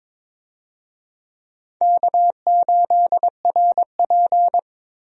Morse code, heard as a keyed tone, about 22 wpm, 700 Hz. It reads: K8RP
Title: USA Call Signs - 4 Characters - Repeated 3x - 22wpm